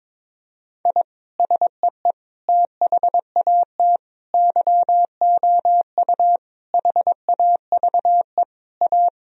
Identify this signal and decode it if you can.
Morse code, heard as a keyed tone, about 22 wpm, 700 Hz. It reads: I SEE THAT YOU HAVE A
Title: Sentences from Top 1000 Words - Review Part 3 - 22wpm